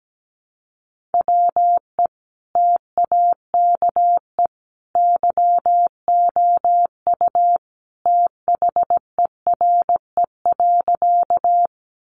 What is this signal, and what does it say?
Morse code, heard as a keyed tone, about 17 wpm, 700 Hz. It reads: WE TAKE YOU THERE.
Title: A Sound Of Thunder 17wpm